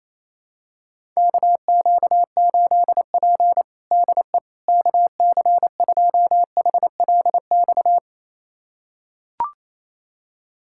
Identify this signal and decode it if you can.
Morse code, heard as a keyed tone, about 28 wpm, 700 Hz. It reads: KQ8P DE KC2HLX
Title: USA Callsign DE Callsign 28wpm